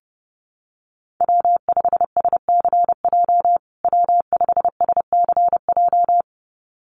Morse code, heard as a keyed tone, about 30 wpm, 700 Hz. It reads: W5SCJ W5SCJ
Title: USA Call Signs Repeated 2x 30wpm